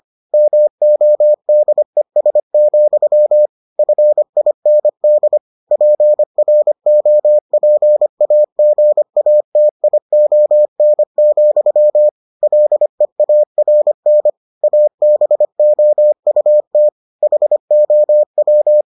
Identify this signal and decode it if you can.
Morse code, heard as a keyed tone, about 25 wpm, 600 Hz. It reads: MODES, FIND PROPAGATION, LEARN ABOUT HOW